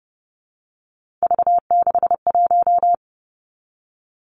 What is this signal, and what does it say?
Morse code, heard as a keyed tone, about 30 wpm, 700 Hz. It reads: V61
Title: Three Challenging Letter-Number Combinations 30wpm